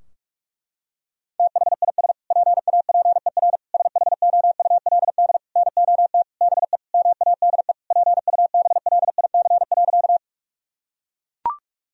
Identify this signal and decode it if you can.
Morse code, heard as a keyed tone, about 41 wpm, 700 Hz. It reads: THIS PAPER SHOULD NOT BE MADE PUBLIC.
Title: Sentences from Top 1000 Words - Review Part 3 - 45wpm